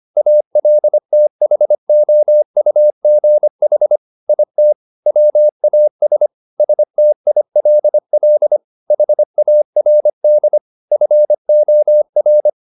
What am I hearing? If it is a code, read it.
Morse code, heard as a keyed tone, about 25 wpm, 600 Hz. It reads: ALTHOUGH IT WAS STILL HARD FOR
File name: the old man and the sea part 1 25wpm